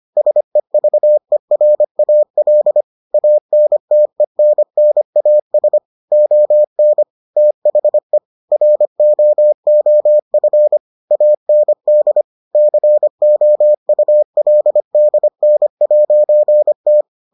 Morse code, heard as a keyed tone, about 25 wpm, 600 Hz. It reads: SEVERAL ANTENNAS ON THE ROOF AND COULDN'T